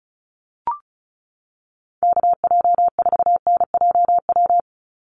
Morse code, heard as a keyed tone, about 34 wpm, 700 Hz. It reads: KJ4NJW
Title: USA Call Signs 35wpm